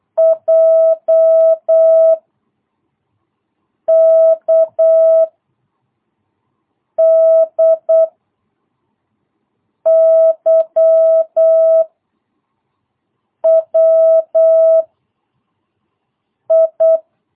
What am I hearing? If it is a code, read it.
Morse code, heard as a keyed tone, about 8 wpm, 660 Hz. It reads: J K D Y W I
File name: jh8cbh / 2022cw